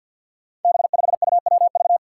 Morse code, heard as a keyed tone, about 53 wpm, 700 Hz. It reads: B5UWV
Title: Five Challenging Letter-Number Combinations 50wpm